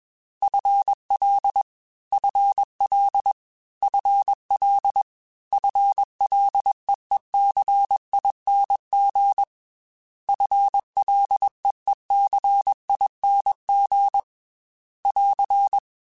Morse code, heard as a keyed tone, about 21 wpm, 800 Hz. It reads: FL FL FL FLEECING FLEECING <AR>
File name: FL / FL mp3s / FL Prefixes 3x2 @20